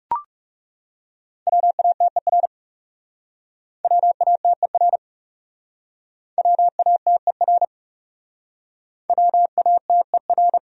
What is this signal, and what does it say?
Morse code, heard as a keyed tone, about 37 wpm, 700 Hz. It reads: WATER WATER WATER WATER
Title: Speed-Racing Top 100 Words 30wpm